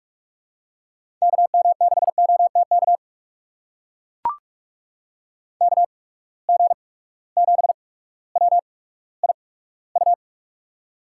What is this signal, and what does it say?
Morse code, heard as a keyed tone, about 44 wpm, spaced as for 19 wpm, 700 Hz. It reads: KM6OTK KG7WIU
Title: USA Call Signs - 6 Characters - 8x Character Spacing - 45wpm